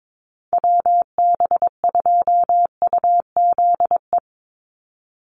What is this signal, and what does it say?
Morse code, heard as a keyed tone, about 22 wpm, 700 Hz. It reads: WB2UZE
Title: Call Signs - 22wpm